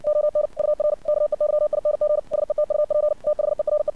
Morse code, heard as a keyed tone, about 59 wpm, 600 Hz. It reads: ON UR KEYING SETUP THERE
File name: archived / CRSnet